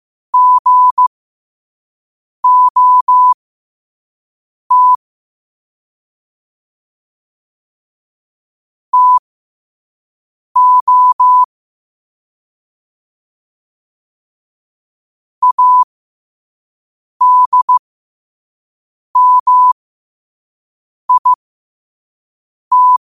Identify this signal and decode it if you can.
Morse code, heard as a keyed tone, about 15 wpm, spaced as for 5 wpm, 1000 Hz. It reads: GOT TO ADMIT
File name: qotdfiles / 5wpm